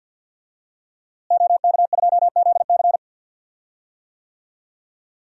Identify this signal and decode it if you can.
Morse code, heard as a keyed tone, about 51 wpm, 700 Hz. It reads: OK2ZC
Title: DXCC-100 Call Signs - 5 Characters - Repeated 2x - 50wpm